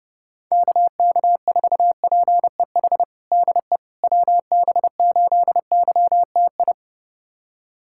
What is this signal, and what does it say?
Morse code, heard as a keyed tone, about 32 wpm, 700 Hz. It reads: KK4PEH DE WB8YTI
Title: USA Callsign DE Callsign - 6 Characters - 30wpm